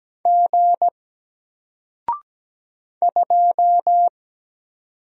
Morse code, heard as a keyed tone, about 17 wpm, 700 Hz. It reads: G 2
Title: Single Letter-Number 17wpm